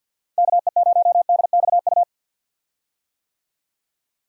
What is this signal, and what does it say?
Morse code, heard as a keyed tone, about 51 wpm, 700 Hz. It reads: KE0DXU
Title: USA Call Signs - 6 Characters - 50wpm